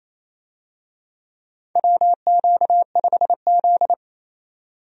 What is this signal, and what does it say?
Morse code, heard as a keyed tone, about 27 wpm, 700 Hz. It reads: WQ5Z
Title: USA Call Signs - 4 Characters - 28wpm